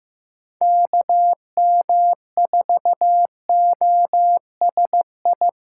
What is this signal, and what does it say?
Morse code, heard as a keyed tone, about 16 wpm, 700 Hz. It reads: KM4OSI
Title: USA Call Signs - 6 Characters - 2x Character Spacing - 15wpm